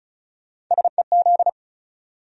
Morse code, heard as a keyed tone, about 49 wpm, spaced as for 42 wpm, 700 Hz. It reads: SEZ
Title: Speed-Racing Abbreviations - 35wpm